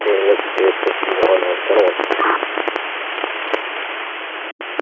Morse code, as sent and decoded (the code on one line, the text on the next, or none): none